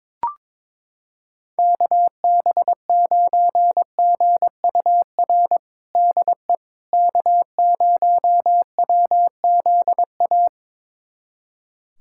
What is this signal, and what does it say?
Morse code, heard as a keyed tone, about 22 wpm, 700 Hz. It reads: KB9GUR DE K0WZA
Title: USA Callsign DE Callsign 22wpm